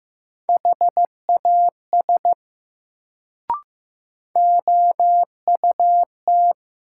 Morse code, heard as a keyed tone, about 15 wpm, 700 Hz. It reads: HAS OUT
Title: Sentences from Top 100 Words - New Word Drill - 15wpm